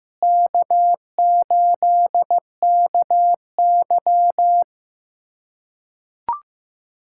Morse code, heard as a keyed tone, about 15 wpm, 700 Hz. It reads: K8KY
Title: USA Callsign DE Callsign 15wpm